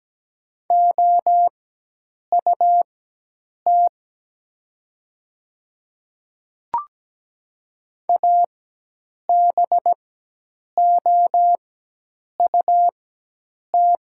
Morse code, heard as a keyed tone, about 17 wpm, spaced as for 8 wpm, 700 Hz. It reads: OUT ABOUT
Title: 4x Character Spacing - Top 100 Words - 17wpm